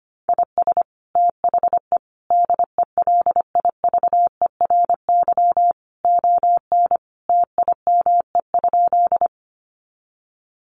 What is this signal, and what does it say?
Morse code, heard as a keyed tone, about 25 wpm, 700 Hz. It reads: IS THE DELIVERY ON TIME?
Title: Sentences from Top 1500 Words 25wpm